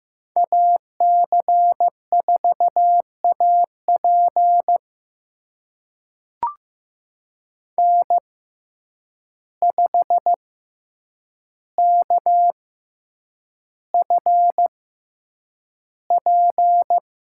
Morse code, heard as a keyed tone, about 15 wpm, spaced as for 8 wpm, 700 Hz. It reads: AC4AP N5KFP
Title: USA Call Signs - 5 Characters - 6x Character Spacing - 15wpm